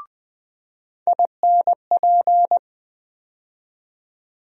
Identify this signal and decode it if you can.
Morse code, heard as a keyed tone, about 20 wpm, 700 Hz. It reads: INP
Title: USA Call Sign Postfixes - 3 Letters 20wpm